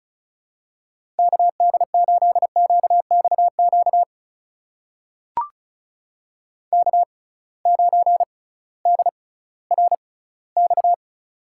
Morse code, heard as a keyed tone, about 34 wpm, spaced as for 19 wpm, 700 Hz. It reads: KD8QXQ K9DRX
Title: USA Call Signs - 6x Character Spacing - 35wpm